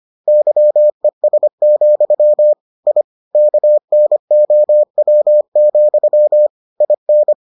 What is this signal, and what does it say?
Morse code, heard as a keyed tone, about 25 wpm, 600 Hz. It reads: YES, I KNOW, IN